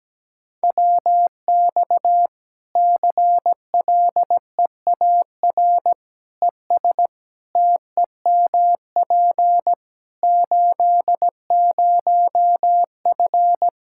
Morse code, heard as a keyed tone, about 17 wpm, 700 Hz. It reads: WX CLEAR ES TEMP 80F
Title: K1USN SST QSO Practice 17wpm